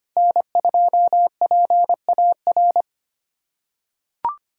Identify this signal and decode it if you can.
Morse code, heard as a keyed tone, about 25 wpm, 700 Hz. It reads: N2PAR
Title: USA Callsign DE Callsign - 5 Characters - 25wpm